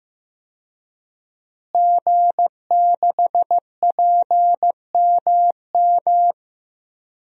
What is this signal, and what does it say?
Morse code, heard as a keyed tone, about 15 wpm, 700 Hz. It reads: G6PMM
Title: DXCC-100 Call Signs - 5 Characters - Repeated 2x - 15wpm